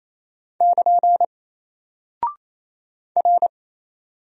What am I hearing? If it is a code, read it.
Morse code, heard as a keyed tone, about 28 wpm, 700 Hz. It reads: <KN> R
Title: Single Character (Letter:Number:Punctuation:Prosign) 28wpm